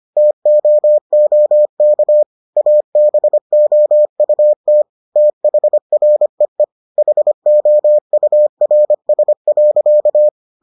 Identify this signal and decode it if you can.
Morse code, heard as a keyed tone, about 25 wpm, 600 Hz. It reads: TOOK ABOUT THREE HOURS.